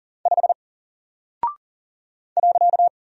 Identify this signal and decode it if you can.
Morse code, heard as a keyed tone, about 40 wpm, 700 Hz. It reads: 5 .
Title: Single Character (Letter:Number:Punctuation:Prosign) 40wpm